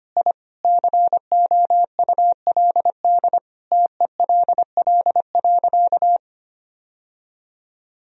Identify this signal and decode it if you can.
Morse code, heard as a keyed tone, about 25 wpm, 700 Hz. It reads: I COULD TELL.
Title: Sentences from Top 1000 Words - Review Part 4 - 25wpm